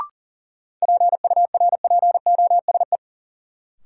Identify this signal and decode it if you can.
Morse code, heard as a keyed tone, about 40 wpm, 700 Hz. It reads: PURPOSE
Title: Sentences from Top 800 Words - New Word Drill - 40wpm